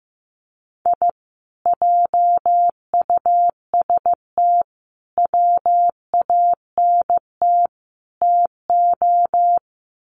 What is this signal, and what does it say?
Morse code, heard as a keyed tone, about 15 wpm, 700 Hz. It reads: I JUST WANT TO
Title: Sets of 4 Words 15wpm